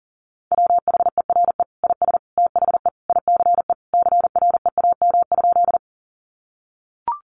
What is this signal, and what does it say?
Morse code, heard as a keyed tone, about 40 wpm, 700 Hz. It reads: WHERE IS THE ICE CREAM?
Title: Sentences from Top 1500 Words 40wpm